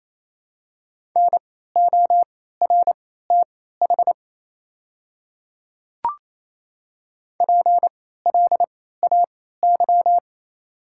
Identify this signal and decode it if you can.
Morse code, heard as a keyed tone, about 28 wpm, spaced as for 16 wpm, 700 Hz. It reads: NORTH PLAY
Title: 3x Character Spacing - Top 300 Words - 28wpm